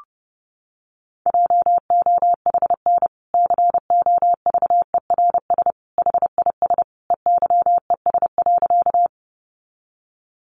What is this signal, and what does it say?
Morse code, heard as a keyed tone, about 30 wpm, 700 Hz. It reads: JOHN COVERS HIS EYES.
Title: Easy Phrases 30wpm